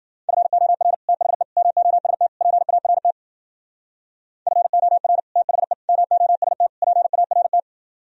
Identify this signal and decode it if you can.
Morse code, heard as a keyed tone, about 51 wpm, 700 Hz. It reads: FOR THE MOST PART FOR THE MOST PART